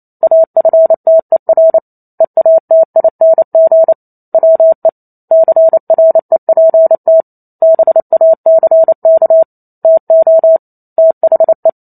Morse code, heard as a keyed tone, about 28 wpm, 650 Hz. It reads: AFTER EATING WE CREPT BACK TO THE